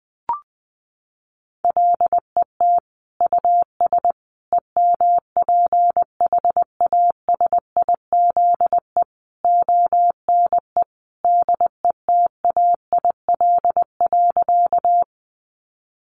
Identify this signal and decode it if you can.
Morse code, heard as a keyed tone, about 20 wpm, 700 Hz. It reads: LET US EMPHASIZE ONE DETAIL.
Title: Sentences from Top 1500 Words 20wpm